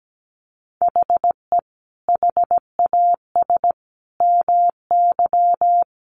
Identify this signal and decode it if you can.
Morse code, heard as a keyed tone, about 17 wpm, 700 Hz. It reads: HE HAS MY
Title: Sentences from Top 1500 Words 17wpm